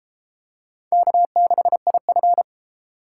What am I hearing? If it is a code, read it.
Morse code, heard as a keyed tone, about 32 wpm, 700 Hz. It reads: K6IF